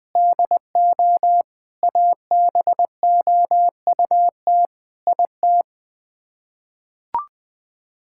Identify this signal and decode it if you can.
Morse code, heard as a keyed tone, about 21 wpm, 700 Hz. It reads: DO ABOUT IT
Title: Sets of 8 Words - 20wpm